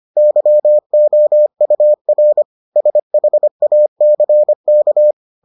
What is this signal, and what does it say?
Morse code, heard as a keyed tone, about 25 wpm, 600 Hz. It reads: YOUR SHACK